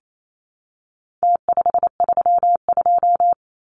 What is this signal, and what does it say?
Morse code, heard as a keyed tone, about 28 wpm, 700 Hz. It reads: T532
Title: Numbers - Contest Style - 4-Digit 28wpm